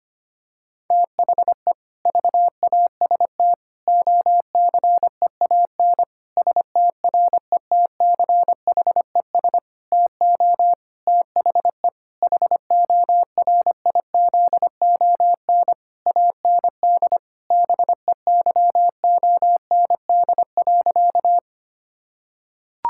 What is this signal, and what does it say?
Morse code, heard as a keyed tone, about 25 wpm, 700 Hz. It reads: THE VAST OCEAN STRETCHES TO THE HORIZON AND BEYOND.
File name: practice / 25wpm / Difficult Sentences with Hints - 25wpm